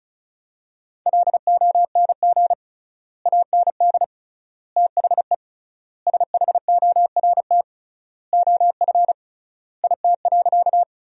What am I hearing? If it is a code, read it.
Morse code, heard as a keyed tone, about 36 wpm, 700 Hz. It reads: LONG AND THE SHORT OF IT.
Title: 3x Word-Spacing - Sentences from Top 400 Words - 35wpm